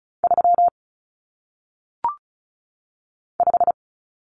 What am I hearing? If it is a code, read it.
Morse code, heard as a keyed tone, about 36 wpm, 700 Hz. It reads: 3 5
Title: Numbers - 1-Digit 35wpm